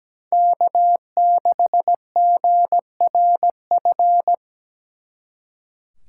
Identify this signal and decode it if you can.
Morse code, heard as a keyed tone, about 20 wpm, 700 Hz. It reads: K6GRF
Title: Speed Racing - USA Call Signs 17wpm